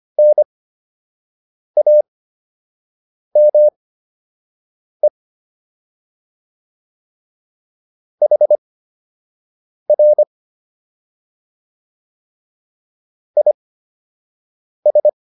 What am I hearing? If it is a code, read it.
Morse code, heard as a keyed tone, about 25 wpm, spaced as for 6 wpm, 600 Hz. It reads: NAME HR IS
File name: QSO 203 06